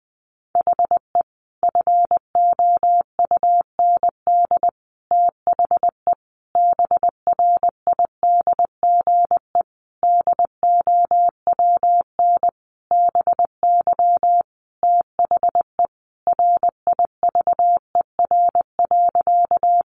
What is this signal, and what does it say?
Morse code, heard as a keyed tone, about 19 wpm, 700 Hz. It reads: HE FOUND THE BRIDGE DOWN BY THE RIVER.
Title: Sentences from Top 1200 Words 20wpm